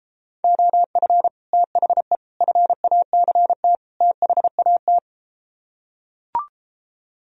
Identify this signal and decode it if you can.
Morse code, heard as a keyed tone, about 33 wpm, 700 Hz. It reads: OF THE FACT THAT